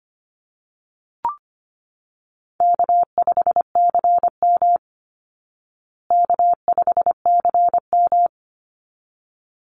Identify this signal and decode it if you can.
Morse code, heard as a keyed tone, about 25 wpm, 700 Hz. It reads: K5CM K5CM
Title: Call Signs Repeated 2x - 25wpm